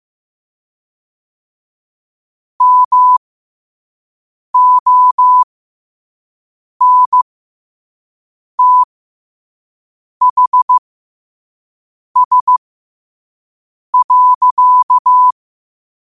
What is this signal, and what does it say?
Morse code, heard as a keyed tone, about 15 wpm, spaced as for 5 wpm, 1000 Hz. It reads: MONTHS.
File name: qotdfiles / 5wpm